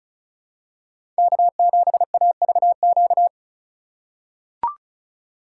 Morse code, heard as a keyed tone, about 34 wpm, 700 Hz. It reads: K7AVQ
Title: USA Call Signs - 5 Characters - 2x Character Spacing - 35wpm